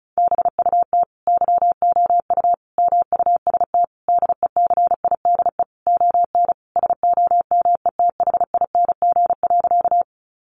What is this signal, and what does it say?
Morse code, heard as a keyed tone, about 34 wpm, 700 Hz. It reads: BUT YOU MUST DECIDE ON SOMETHING.
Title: Sentences from Top 600 Words 35wpm